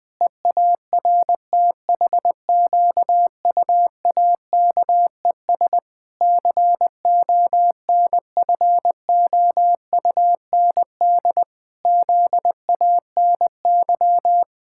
Morse code, heard as a keyed tone, about 20 wpm, 700 Hz. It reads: EARTHQUAKES CONFOUND ZANY
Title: Pangram Challenge v2 - 20wpm